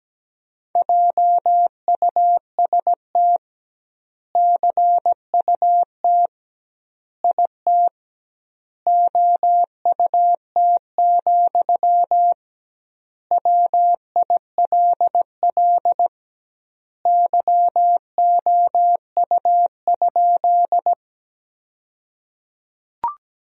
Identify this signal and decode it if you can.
Morse code, heard as a keyed tone, about 17 wpm, 700 Hz. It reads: JUST CUT IT OUT, WILL YOU?
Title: 2x Word-Spacing - Sentences from Top 300 Words - 17wpm